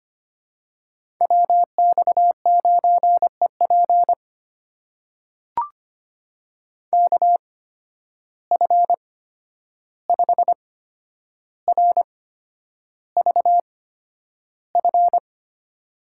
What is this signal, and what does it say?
Morse code, heard as a keyed tone, about 25 wpm, spaced as for 10 wpm, 700 Hz. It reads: WX9EP KF5RVF
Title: USA Call Signs - 8x Character Spacing - 25wpm